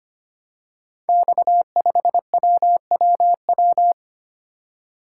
Morse code, heard as a keyed tone, about 25 wpm, 700 Hz. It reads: X5WWW
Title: Code Groups - 5 Letter:Numbers - 25wpm